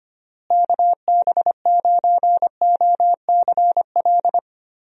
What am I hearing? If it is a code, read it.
Morse code, heard as a keyed tone, about 25 wpm, 700 Hz. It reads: KB9OCL
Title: USA Call Signs - 6 Characters - Repeated 3x - 25wpm